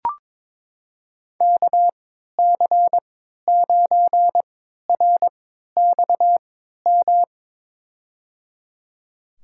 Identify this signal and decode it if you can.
Morse code, heard as a keyed tone, about 22 wpm, spaced as for 13 wpm, 700 Hz. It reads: KC9RXM
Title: USA Call Signs - 6 Characters - 3x Character Spacing - 22wpm